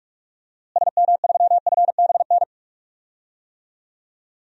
Morse code, heard as a keyed tone, about 47 wpm, 700 Hz. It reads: SM3FBN
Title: DXCC-100 Call Signs - 6 Characters - Repeated 2x - 45wpm